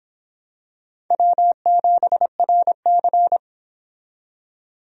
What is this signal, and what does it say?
Morse code, heard as a keyed tone, about 26 wpm, 700 Hz. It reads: W7RC